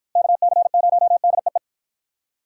Morse code, heard as a keyed tone, about 52 wpm, 700 Hz. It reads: KC0DEE